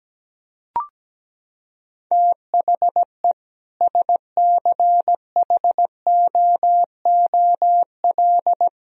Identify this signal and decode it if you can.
Morse code, heard as a keyed tone, about 17 wpm, 700 Hz. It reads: THE SCHOOL
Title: Sentences from Top 1000 Words - Review Part 2 - 17wpm